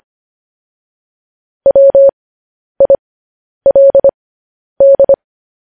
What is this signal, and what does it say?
Morse code, heard as a keyed tone, about 25 wpm, spaced as for 10 wpm, 550 Hz. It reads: WILD
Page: News Headlines in Morse Code at 10 WPM